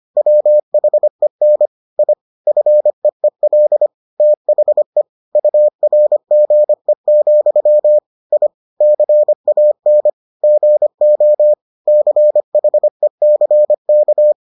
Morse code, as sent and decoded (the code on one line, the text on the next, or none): .-- .... -. / .. / ..-. . .-.. / - .... / ..- .-. --. --..-- / .. / -.-. .- -. / --. --- / -.-. .... -.-. -.-
WHEN I FEEL THE URGE, I CAN GO CHECK